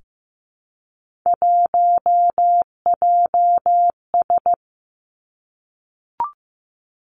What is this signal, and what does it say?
Morse code, heard as a keyed tone, about 15 wpm, 700 Hz. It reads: 1JS
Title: Three Challenging Letter-Number Combinations 15wpm